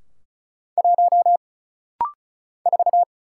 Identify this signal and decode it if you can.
Morse code, heard as a keyed tone, about 34 wpm, 700 Hz. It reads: 1 4
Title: Single Letter-Number Alt - Rapid-Fire 35wpm